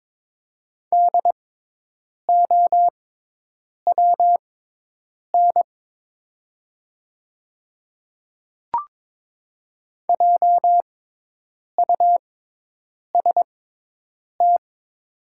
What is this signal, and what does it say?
Morse code, heard as a keyed tone, about 22 wpm, spaced as for 8 wpm, 700 Hz. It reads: DOWN JUST
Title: 6x Character Spacing - Top 200 Words - 22wpm